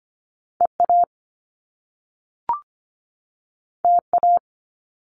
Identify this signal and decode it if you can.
Morse code, heard as a keyed tone, about 25 wpm, 700 Hz. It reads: EA TA
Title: Top Two Letter Combinations in Words 25wpm